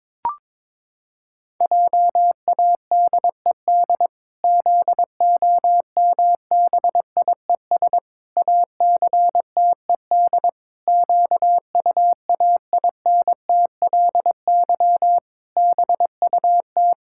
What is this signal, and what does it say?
Morse code, heard as a keyed tone, about 22 wpm, 700 Hz. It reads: JADED ZOMBIES ACTED QUAINTLY BUT
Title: Pangram Challenge v2 - 22wpm